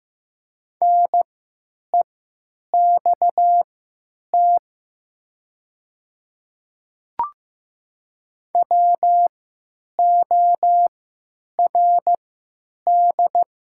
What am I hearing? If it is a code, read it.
Morse code, heard as a keyed tone, about 15 wpm, spaced as for 9 wpm, 700 Hz. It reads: NEXT WORD
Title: 3x Character Spacing - Top 300 Words - 15wpm